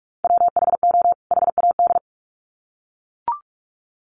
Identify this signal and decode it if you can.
Morse code, heard as a keyed tone, about 44 wpm, 700 Hz. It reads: WHO HAD
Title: Sets of 2 Words 45wpm